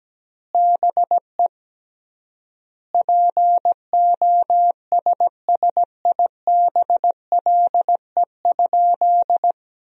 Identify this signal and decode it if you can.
Morse code, heard as a keyed tone, about 17 wpm, 700 Hz. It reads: BE POSSIBLE?
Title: 3x Word-Spacing - Sentences from Top 500 Words - 17wpm